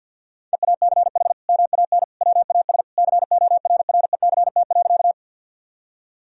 Morse code, heard as a keyed tone, about 50 wpm, 700 Hz. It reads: EACH MAN WAS CORRECT.
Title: Sentences from Top 1000 Words - Review Part 2 - 50wpm